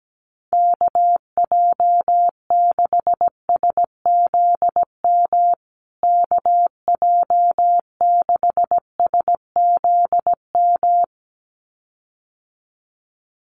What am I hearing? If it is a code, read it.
Morse code, heard as a keyed tone, about 17 wpm, 700 Hz. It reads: KJ6SZM KJ6SZM
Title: USA Call Signs Repeated 2x 17wpm